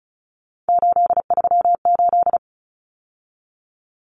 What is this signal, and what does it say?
Morse code, heard as a keyed tone, about 34 wpm, 700 Hz. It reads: 838
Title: Numbers - 3-Digit 35wpm